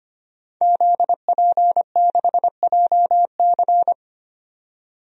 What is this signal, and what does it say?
Morse code, heard as a keyed tone, about 25 wpm, 700 Hz. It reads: ZP6JC
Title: DXCC-100 Call Signs - 5 Characters - 25wpm